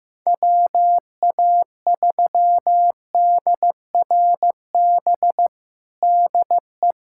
Morse code, heard as a keyed tone, about 16 wpm, 700 Hz. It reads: WA3DRB DE
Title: USA Callsign DE Callsign - 6 Characters - 15wpm